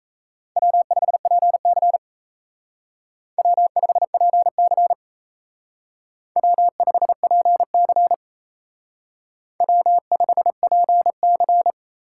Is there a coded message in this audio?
Yes: Morse code from 0.6 to 11.7 s, heard as a keyed tone, about 35 wpm, 700 Hz. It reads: W5PC W5PC W5PC W5PC